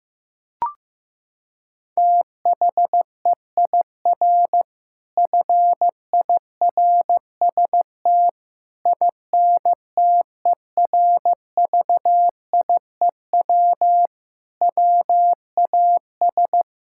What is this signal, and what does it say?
Morse code, heard as a keyed tone, about 15 wpm, 700 Hz. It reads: THEIR FIRST INTERVIEW WAS
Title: Sentences from Top 1600 Words 15wpm